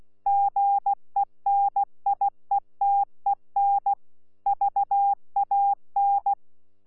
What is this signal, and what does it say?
Morse code, heard as a keyed tone, about 16 wpm, 800 Hz. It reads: GENIETEN VAN
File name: Z PI4VRZ / PI4VRZ 14-jun-2025